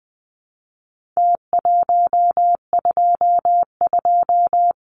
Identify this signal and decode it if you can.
Morse code, heard as a keyed tone, about 20 wpm, 700 Hz. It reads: T122
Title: Numbers - Contest Style - 4-Digit 20wpm